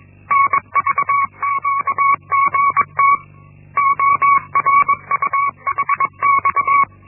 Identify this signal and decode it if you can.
Morse code, heard as a keyed tone, about 21 wpm, 1120 Hz. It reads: NVQGT ORUHX
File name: Linksmakalnis-1